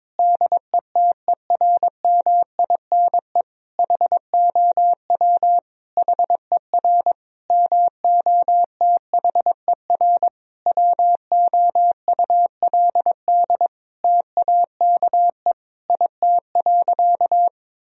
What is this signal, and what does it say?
Morse code, heard as a keyed tone, about 22 wpm, 700 Hz. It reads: DETERMINE HOW HER MOTHER WOULD TAKE IT.
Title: Sentences from Top 1000 Words - Review Part 4 - 22wpm